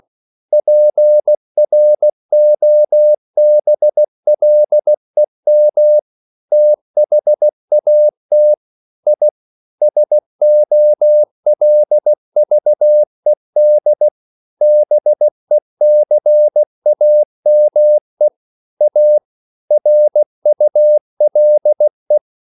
Morse code, heard as a keyed tone, about 16 wpm, 600 Hz. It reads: PROBLEM THAT I SOLVED BECAME A RULE
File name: CW / Practice / EnglishQuotations0116